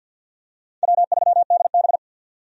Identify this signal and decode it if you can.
Morse code, heard as a keyed tone, about 52 wpm, 700 Hz. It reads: W3DB
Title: USA Call Signs - 4 Characters - 50wpm